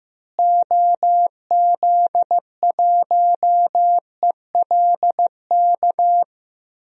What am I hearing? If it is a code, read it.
Morse code, heard as a keyed tone, about 15 wpm, 700 Hz. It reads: OZ1ELK
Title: DXCC-100 Call Signs - 6 Characters - Repeated 2x - 15wpm